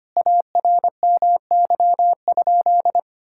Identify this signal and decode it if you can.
Morse code, heard as a keyed tone, about 25 wpm, 700 Hz. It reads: ARMY?
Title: Sentences from Top 1000 Words - Review Part 6 - 25wpm